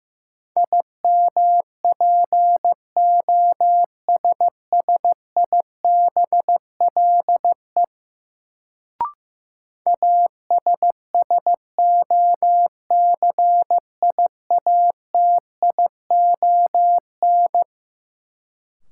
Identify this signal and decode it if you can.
Morse code, heard as a keyed tone, about 15 wpm, 700 Hz. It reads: IMPOSSIBLE ASSOCIATION
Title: Sentences from Top 1100 Words - New Word Drill Alt - 15wpm